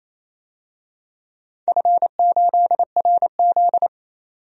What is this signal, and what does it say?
Morse code, heard as a keyed tone, about 28 wpm, 700 Hz. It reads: F8RZ
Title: DXCC-100 Call Signs - 4 Characters - 28wpm